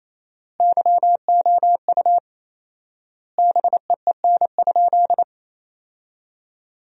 Morse code, heard as a keyed tone, about 28 wpm, 700 Hz. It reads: YOU BEEN?
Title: 4x Word-Spacing - Sentences from Top 200 Words - 28wpm